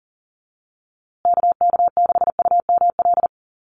Morse code, heard as a keyed tone, about 40 wpm, 700 Hz. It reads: KK6UML
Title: USA Call Signs Repeated 3x 40wpm